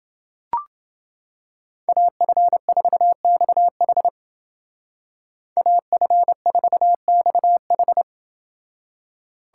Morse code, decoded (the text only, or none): AF4XH AF4XH